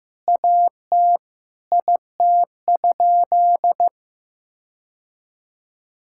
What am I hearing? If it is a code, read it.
Morse code, heard as a keyed tone, about 16 wpm, 700 Hz. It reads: AT IT?
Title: Sentences from Top 1600 Words 15wpm